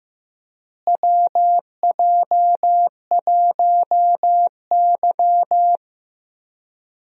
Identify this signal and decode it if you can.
Morse code, heard as a keyed tone, about 15 wpm, 700 Hz. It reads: WJ1Y
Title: USA Call Signs - 4 Characters - Repeated 3x - 15wpm